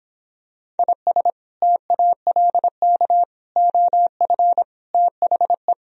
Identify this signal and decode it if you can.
Morse code, heard as a keyed tone, about 26 wpm, 700 Hz. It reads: IS TALK OF THE